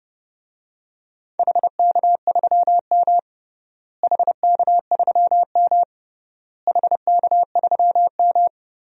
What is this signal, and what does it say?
Morse code, heard as a keyed tone, about 30 wpm, 700 Hz. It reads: HK3M HK3M HK3M
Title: DXCC-100 Call Signs - 4 Characters - Repeated 3x - 30wpm